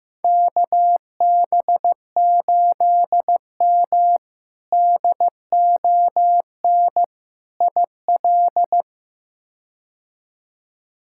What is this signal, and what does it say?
Morse code, heard as a keyed tone, about 15 wpm, 700 Hz. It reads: KB8M DON IL
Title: K1USN SST QSO Practice - 15wpm